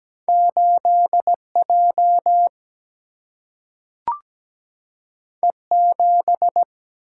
Morse code, heard as a keyed tone, about 17 wpm, 700 Hz. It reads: 8 J E 7
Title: Single-Letter Single-Number 17wpm